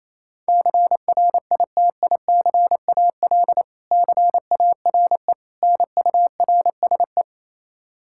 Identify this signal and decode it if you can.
Morse code, heard as a keyed tone, about 28 wpm, 700 Hz. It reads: CRITICAL CARE NURSE
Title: Practice QSO - Exchange 3 - More Info - 28wpm